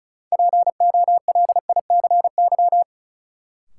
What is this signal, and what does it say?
Morse code, heard as a keyed tone, about 36 wpm, 700 Hz. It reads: POLICY
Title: Sentences from Top 1000 Words - Review Part 5 - New Word Drill - 35wpm